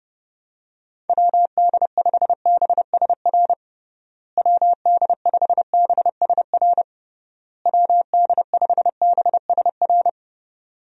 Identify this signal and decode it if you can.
Morse code, heard as a keyed tone, about 32 wpm, 700 Hz. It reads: WD5BSR WD5BSR WD5BSR
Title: USA Call Signs - 6 Characters - Repeated 3x - 30wpm